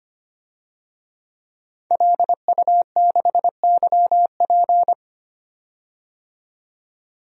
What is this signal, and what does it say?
Morse code, heard as a keyed tone, about 25 wpm, 700 Hz. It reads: LU6YP
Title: DXCC-100 Call Signs - 5 Characters - 25wpm